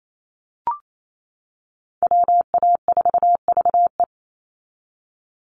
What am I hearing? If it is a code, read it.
Morse code, heard as a keyed tone, about 28 wpm, 700 Hz. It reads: WA4VE
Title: USA Call Signs 28wpm